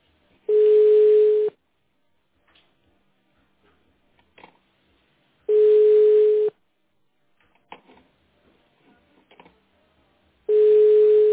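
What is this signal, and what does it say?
Ringback tone (Europe, 425 Hz).
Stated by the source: Call progress tone in a phone call recorded on the petrosvyaz.mdpbx PBX.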